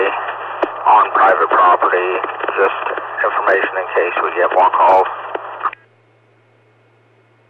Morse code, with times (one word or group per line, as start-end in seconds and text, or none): none